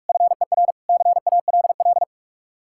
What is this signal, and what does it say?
Morse code, heard as a keyed tone, about 61 wpm, spaced as for 53 wpm, 700 Hz. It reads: FER CALL
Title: Speed-Racing Common QSO Phrases - 45wpm